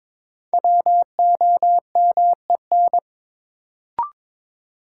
Morse code, heard as a keyed tone, about 21 wpm, 700 Hz. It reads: WOMEN
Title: Sets of 3 Words - New Word Drill - 22wpm